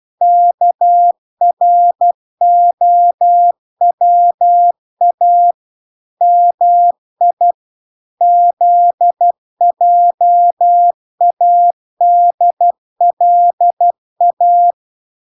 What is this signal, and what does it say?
Morse code, heard as a keyed tone, about 12 wpm, 700 Hz. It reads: KROWA MI ZJADLA